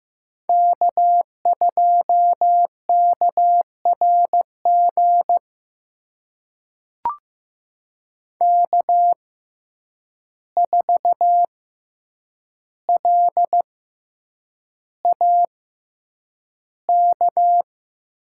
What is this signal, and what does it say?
Morse code, heard as a keyed tone, about 15 wpm, spaced as for 8 wpm, 700 Hz. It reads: K2KRG K4LAK
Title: USA Call Signs - 5 Characters - 6x Character Spacing - 15wpm